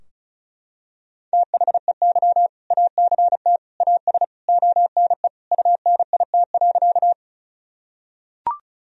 Morse code, heard as a keyed tone, about 34 wpm, 700 Hz. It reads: THEY ACT AS ONE UNIT.
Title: Sentences from Top 1000 Words - Review Part 2 - 35wpm